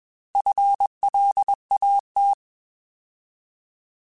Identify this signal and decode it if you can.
Morse code, heard as a keyed tone, about 22 wpm, 800 Hz. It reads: FLAT